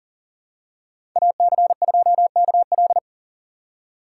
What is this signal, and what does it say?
Morse code, heard as a keyed tone, about 40 wpm, 700 Hz. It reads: AC2KL
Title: USA Call Signs - 5 Characters - 3x Character Spacing - 40wpm